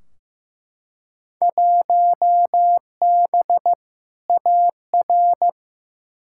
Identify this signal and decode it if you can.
Morse code, heard as a keyed tone, about 15 wpm, 700 Hz. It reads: 1B AR
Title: Field Day Exchanges Encore - 15wpm